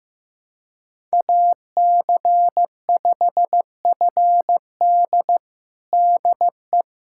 Morse code, heard as a keyed tone, about 15 wpm, 700 Hz. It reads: AC5FD DE
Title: USA Callsign DE Callsign - 5 Characters - 15wpm